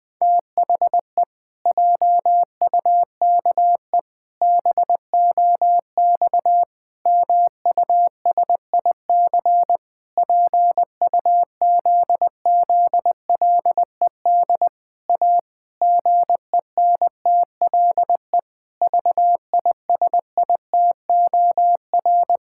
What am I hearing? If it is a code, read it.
Morse code, heard as a keyed tone, about 20 wpm, 700 Hz. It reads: THE JUKE BOX MUSIC PUZZLED A GENTLE VISITOR
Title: Pangram Challenge v2 - 20wpm